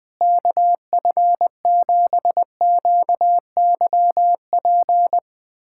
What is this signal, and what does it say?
Morse code, heard as a keyed tone, about 21 wpm, 700 Hz. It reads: KF7QYP
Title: USA Callsign DE Callsign - 6 Characters - 20wpm